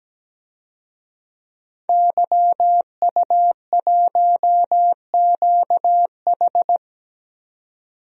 Morse code, heard as a keyed tone, about 17 wpm, 700 Hz. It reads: YU1QH
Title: DXCC-100 Call Signs - 5 Characters - 17wpm